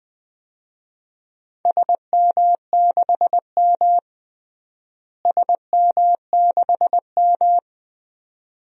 Morse code, heard as a keyed tone, about 20 wpm, 700 Hz. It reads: SM6M SM6M
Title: DXCC-100 Call Signs - 4 Characters - Repeated 2x - 20wpm